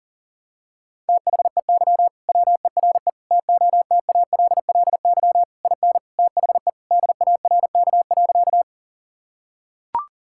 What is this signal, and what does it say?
Morse code, heard as a keyed tone, about 40 wpm, 700 Hz. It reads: THEY WERE TOTALLY IN THE DARK.
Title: Sentences from Top 1600 Words 40wpm